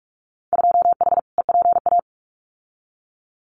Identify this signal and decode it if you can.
Morse code, heard as a keyed tone, about 44 wpm, 700 Hz. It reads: 2H EPA
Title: Winter Field Day Exchanges 45wpm